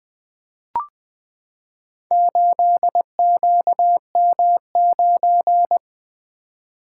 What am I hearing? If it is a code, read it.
Morse code, heard as a keyed tone, about 20 wpm, 700 Hz. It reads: 8QM9
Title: Four Letter-Number Combinations - 20wpm